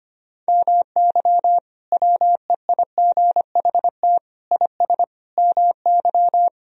Morse code, heard as a keyed tone, about 25 wpm, 700 Hz. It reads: MY WEIGHT IS MY
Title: Sentences from Top 1000 Words - Review Part 3 - 25wpm